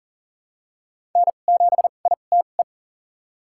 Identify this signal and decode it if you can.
Morse code, heard as a keyed tone, about 39 wpm, spaced as for 28 wpm, 700 Hz. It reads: N7ITE
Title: USA Call Signs - 5 Characters - 2x Character Spacing - 40wpm